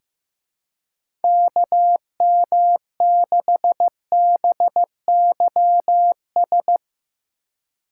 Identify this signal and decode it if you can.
Morse code, heard as a keyed tone, about 15 wpm, 700 Hz. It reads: KM6BYS
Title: USA Call Signs - 6 Characters - 2x Character Spacing - 15wpm